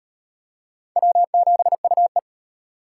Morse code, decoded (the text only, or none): W7UE